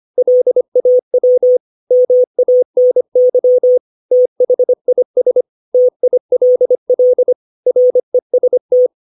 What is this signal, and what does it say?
Morse code, heard as a keyed tone, about 25 wpm, 500 Hz. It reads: LAW MANY THIS TILL REST
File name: 632 words / 632 Words 25WPM